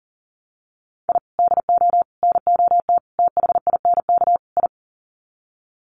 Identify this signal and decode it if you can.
Morse code, heard as a keyed tone, about 40 wpm, 700 Hz. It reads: I DO NOT THINK I
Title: Sets of 5 Words 40wpm